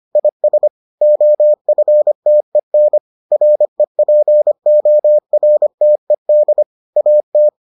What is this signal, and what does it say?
Morse code, heard as a keyed tone, about 25 wpm, 600 Hz. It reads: IS OFTEN REPORTED AT